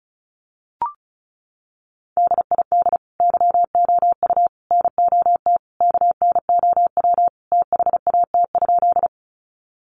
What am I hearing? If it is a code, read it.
Morse code, heard as a keyed tone, about 34 wpm, 700 Hz. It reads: DID YOU NOT KNOW THAT?
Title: Sentences from Top 600 Words 35wpm